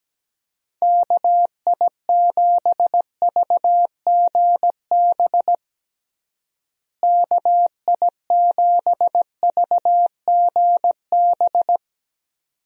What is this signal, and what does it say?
Morse code, heard as a keyed tone, about 17 wpm, 700 Hz. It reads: KI7VGB KI7VGB
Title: USA Call Signs - 6 Characters - Repeated 2x - 17wpm